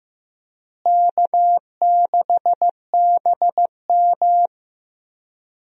Morse code, heard as a keyed tone, about 15 wpm, 700 Hz. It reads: K6BM
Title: USA Call Signs - 4 Characters - Repeated 2x - 15wpm